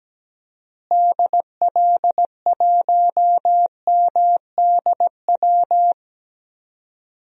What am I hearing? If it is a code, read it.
Morse code, heard as a keyed tone, about 17 wpm, 700 Hz. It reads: DL1MDW
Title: DXCC-100 Call Signs - 6 Characters - Repeated 2x - 17wpm